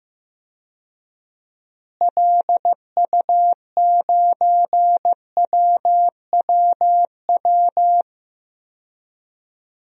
Morse code, heard as a keyed tone, about 15 wpm, 700 Hz. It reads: LU9WWW
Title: DXCC-100 Call Signs - 6 Characters - Repeated 2x - 15wpm